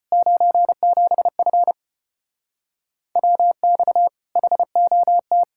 Morse code, heard as a keyed tone, about 33 wpm, 700 Hz. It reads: 97F WX HOT